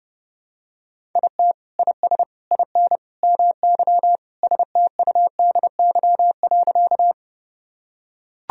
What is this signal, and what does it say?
Morse code, heard as a keyed tone, about 32 wpm, 700 Hz. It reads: IT IS IN MY STUDY.
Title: Sentences from Top 1000 Words - Review Part 1 - 30wpm